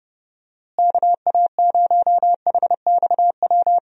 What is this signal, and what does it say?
Morse code, heard as a keyed tone, about 32 wpm, 700 Hz. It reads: KA0HXW
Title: USA Call Signs - 6 Characters - Repeated 2x - 30wpm